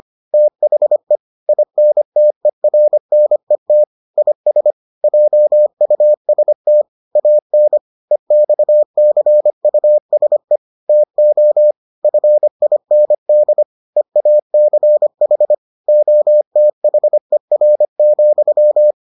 Morse code, heard as a keyed tone, about 25 wpm, 600 Hz. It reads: THE INTERNET IS JUST AN EXCUSE TO FIND EACH OTHER,